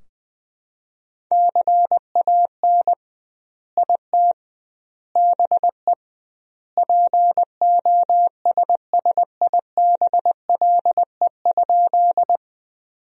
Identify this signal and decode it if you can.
Morse code, heard as a keyed tone, about 20 wpm, 700 Hz. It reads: CAN IT BE POSSIBLE?
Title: 2x Word-Spacing - Sentences from Top 500 Words - 20wpm